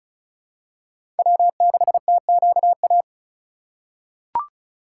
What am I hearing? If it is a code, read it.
Morse code, heard as a keyed tone, about 36 wpm, 700 Hz. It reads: W6TQA
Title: USA Call Signs - 5 Characters - 3x Character Spacing - 35wpm